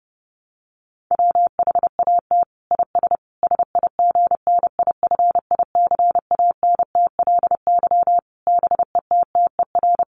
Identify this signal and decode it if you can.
Morse code, heard as a keyed tone, about 30 wpm, 700 Hz. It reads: WHAT IS SIGNIFICANTLY BETTER
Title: Sentences from Top 1500 Words 30wpm